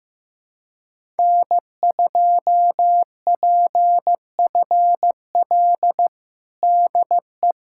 Morse code, heard as a keyed tone, about 15 wpm, 700 Hz. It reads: N2PFL DE
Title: USA Callsign DE Callsign - 5 Characters - 15wpm